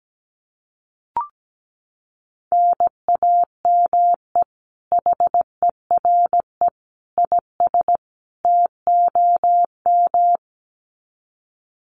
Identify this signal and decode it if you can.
Morse code, heard as a keyed tone, about 17 wpm, 700 Hz. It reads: NAME HERE IS TOM
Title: Common QSO Phrases 17wpm